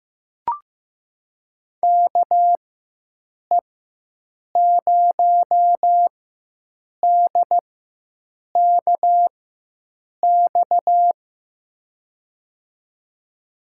Morse code, heard as a keyed tone, about 15 wpm, spaced as for 7 wpm, 700 Hz. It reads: KE0DKX
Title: USA Call Signs - 4x Character Spacing - 15wpm